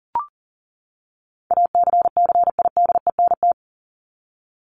Morse code, heard as a keyed tone, about 40 wpm, 700 Hz. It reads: ACCIDENT
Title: Top 1751-2000 Words 40wpm